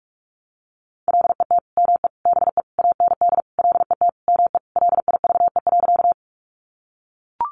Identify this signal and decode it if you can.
Morse code, heard as a keyed tone, about 44 wpm, 700 Hz. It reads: LET ME BE AND LET ME LIVE.
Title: Sentences from Top 300 Words 45wpm